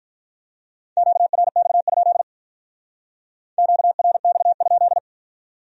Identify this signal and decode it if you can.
Morse code, heard as a keyed tone, about 44 wpm, 700 Hz. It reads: QRX? QRX?